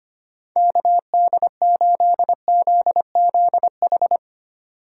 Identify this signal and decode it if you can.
Morse code, heard as a keyed tone, about 25 wpm, 700 Hz. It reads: KD8ZZH
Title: USA Call Signs - 6 Characters - 4x Character Spacing - 25wpm